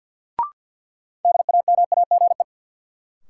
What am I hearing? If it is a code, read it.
Morse code, heard as a keyed tone, about 53 wpm, 700 Hz. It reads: DAMAGE
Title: Sentences from Top 1100 Words - New Word Drill - 50wpm